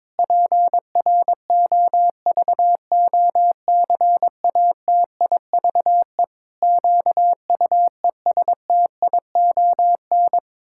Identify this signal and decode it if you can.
Morse code, heard as a keyed tone, about 22 wpm, 700 Hz. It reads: PROVOCATIVE QUESTION
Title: Pangram Challenge v2 - 22wpm